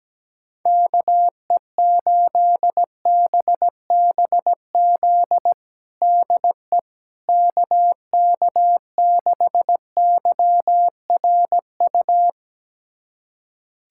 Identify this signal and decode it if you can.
Morse code, heard as a keyed tone, about 17 wpm, 700 Hz. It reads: KE8BBZ DE KK6YRU
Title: USA Callsign DE Callsign - 6 Characters - 17wpm